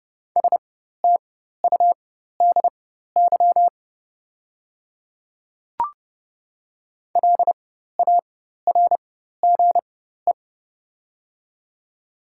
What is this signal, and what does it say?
Morse code, heard as a keyed tone, about 30 wpm, spaced as for 14 wpm, 690 Hz. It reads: STUDY LARGE
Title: 4x Character Spacing - Top 500 Words - 30wpm